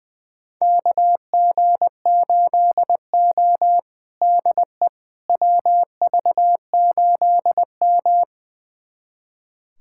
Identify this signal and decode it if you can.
Morse code, heard as a keyed tone, about 20 wpm, 700 Hz. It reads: KG8O DE WV8M
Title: USA Callsign DE Callsign - 4 Characters - 20wpm